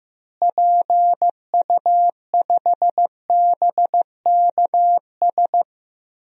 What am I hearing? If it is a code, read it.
Morse code, heard as a keyed tone, about 16 wpm, 700 Hz. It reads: PU5BKS
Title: DXCC-100 Call Signs - 6 Characters - Repeated 3x - 15wpm